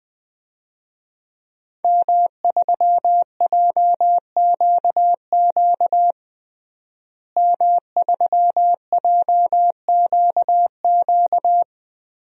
Morse code, heard as a keyed tone, about 21 wpm, 700 Hz. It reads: M3JQQ M3JQQ
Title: DXCC-100 Call Signs - 5 Characters - Repeated 3x - 20wpm